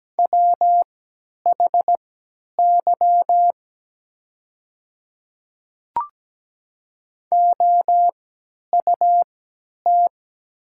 Morse code, heard as a keyed tone, about 17 wpm, spaced as for 10 wpm, 700 Hz. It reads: WHY OUT
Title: 3x Character Spacing - Top 300 Words - 17wpm